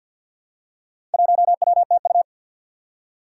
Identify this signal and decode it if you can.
Morse code, heard as a keyed tone, about 52 wpm, 700 Hz. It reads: 1WTU
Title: Four Challenging Letter-Number Combinations 50wpm